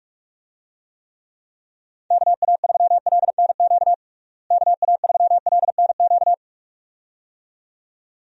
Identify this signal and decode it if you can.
Morse code, heard as a keyed tone, about 46 wpm, 700 Hz. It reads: KA3LNQ KA3LNQ
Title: USA Call Signs - 6 Characters - Repeated 2x - 45wpm